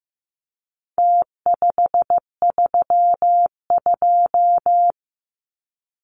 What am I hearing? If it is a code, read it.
Morse code, heard as a keyed tone, about 15 wpm, 700 Hz. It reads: T532
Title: Numbers - Contest Style - 4-Digit 15wpm